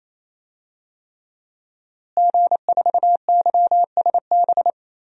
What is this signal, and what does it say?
Morse code, heard as a keyed tone, about 27 wpm, 700 Hz. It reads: G4YSB
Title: DXCC-100 Call Signs - 5 Characters - Repeated 3x - 28wpm